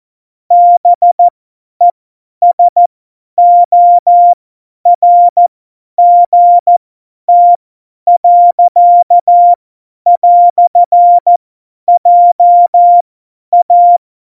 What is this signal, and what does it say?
Morse code, heard as a keyed tone, about 14 wpm, spaced as for 10 wpm, 700 Hz. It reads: BESORGT. "JA
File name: Morsekurs / Witze / Witz 17. Oktober 17, Tempo 50